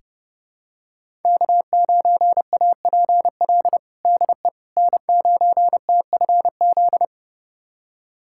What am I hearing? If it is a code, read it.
Morse code, heard as a keyed tone, about 30 wpm, 700 Hz. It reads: K9APL DE N9TFZ
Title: USA Callsign DE Callsign - 5 Characters - 30wpm